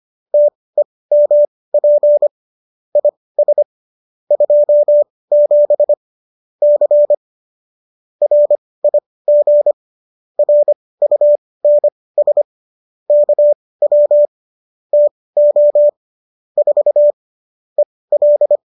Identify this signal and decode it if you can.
Morse code, heard as a keyed tone, about 25 wpm, spaced as for 18 wpm, 590 Hz. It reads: TEMP IS 27 C RIG RUNS KW TO 4 EL
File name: qso207 18